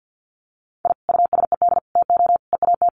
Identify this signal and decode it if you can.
Morse code, heard as a keyed tone, about 50 wpm, 700 Hz. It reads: I USED TO EAT
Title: Sentences from Top 1200 Words 50wpm